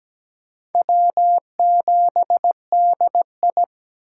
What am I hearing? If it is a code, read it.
Morse code, heard as a keyed tone, about 17 wpm, 700 Hz. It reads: W7DI
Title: USA Call Signs - 3x Character Spacing - 17wpm